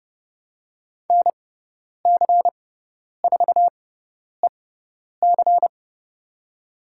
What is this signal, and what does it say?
Morse code, heard as a keyed tone, about 29 wpm, spaced as for 10 wpm, 700 Hz. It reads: NC4EC
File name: Lesson 17g - Letter u - Call Signs 30wpm